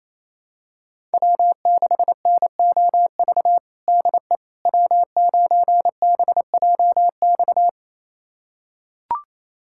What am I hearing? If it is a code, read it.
Morse code, heard as a keyed tone, about 28 wpm, 700 Hz. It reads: W6NOV DE W9BJX
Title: USA Callsign DE Callsign - 5 Characters - 28wpm